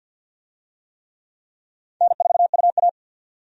Morse code, heard as a keyed tone, about 51 wpm, 700 Hz. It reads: N4UA
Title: USA Call Signs - 4 Characters - 50wpm